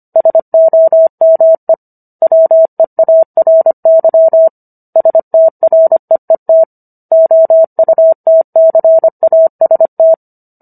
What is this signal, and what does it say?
Morse code, heard as a keyed tone, about 25 wpm, 650 Hz. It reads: SOME WEARY STREET OUTCAST